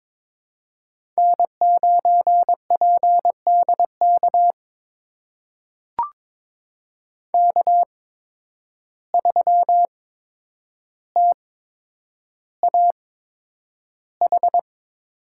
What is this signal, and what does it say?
Morse code, heard as a keyed tone, about 22 wpm, spaced as for 9 wpm, 700 Hz. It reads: N9PDK K3TAH
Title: USA Call Signs - 5 Characters - 8x Character Spacing - 22wpm